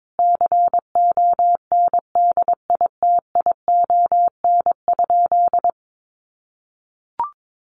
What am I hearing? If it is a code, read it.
Morse code, heard as a keyed tone, about 22 wpm, 700 Hz. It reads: CONDITION?
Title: Sentences from Top 1000 Words 22wpm